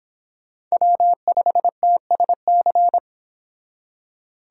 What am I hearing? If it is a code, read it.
Morse code, heard as a keyed tone, about 30 wpm, 700 Hz. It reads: W5TSC